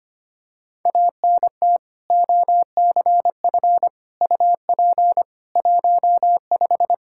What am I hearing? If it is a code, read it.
Morse code, heard as a keyed tone, about 25 wpm, 700 Hz. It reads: ANT OCF UP 15
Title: Practice QSO - Exchange 2 - Info - 25wpm